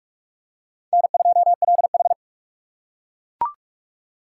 Morse code, heard as a keyed tone, about 38 wpm, 700 Hz. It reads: N2LH